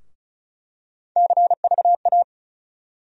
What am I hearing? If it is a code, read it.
Morse code, heard as a keyed tone, about 36 wpm, 700 Hz. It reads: CVA
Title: USA Call Sign Postfixes - 3 Letters 35wpm